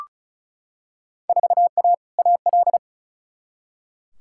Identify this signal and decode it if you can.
Morse code, heard as a keyed tone, about 36 wpm, 700 Hz. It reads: 4A AL
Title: Field Day Exchanges - 35wpm